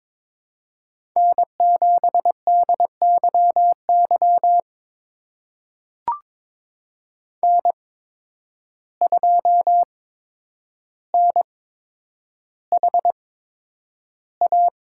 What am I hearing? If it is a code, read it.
Morse code, heard as a keyed tone, about 22 wpm, spaced as for 9 wpm, 700 Hz. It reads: N7DYY N2NHA
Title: USA Call Signs - 5 Characters - 8x Character Spacing - 22wpm